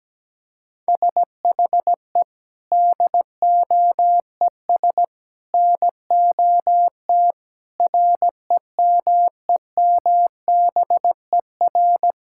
Morse code, heard as a keyed tone, about 17 wpm, 700 Hz. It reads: SHE DOES NOT REMEMBER
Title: Sentences from Top 1000 Words - Review Part 5 - 17wpm